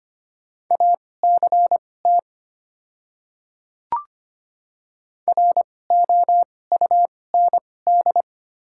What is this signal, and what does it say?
Morse code, heard as a keyed tone, about 25 wpm, spaced as for 18 wpm, 700 Hz. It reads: ACT ROUND
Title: 2x Character Spacing - Top 500 Words - 25wpm